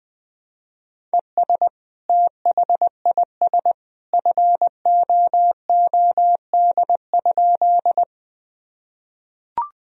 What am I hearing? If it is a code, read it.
Morse code, heard as a keyed tone, about 21 wpm, 700 Hz. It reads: ES THIS FOOD?
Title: Sentences from Top 300 Words and Abbreviations 20wpm